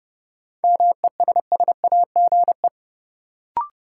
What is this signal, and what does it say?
Morse code, heard as a keyed tone, about 31 wpm, 700 Hz. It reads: MESSAGE
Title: Sentences from Top 1100 Words - New Word Drill - 30wpm